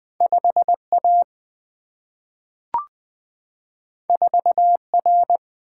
Morse code, heard as a keyed tone, about 20 wpm, 700 Hz. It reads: 5 A 4 R
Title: Single-Letter Single-Number 20wpm